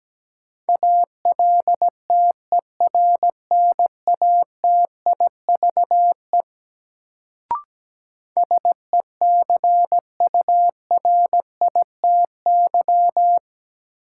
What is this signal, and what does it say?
Morse code, heard as a keyed tone, about 17 wpm, 700 Hz. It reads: ALTERNATIVE SECURITY
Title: Sentences from Top 1000 Words - Review Part 4 - New Word Drill - 17wpm